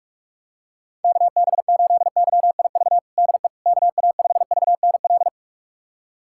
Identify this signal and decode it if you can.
Morse code, heard as a keyed tone, about 46 wpm, 700 Hz. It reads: KB8YIV DE KA5VNL
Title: USA Callsign DE Callsign - 6 Characters - 45wpm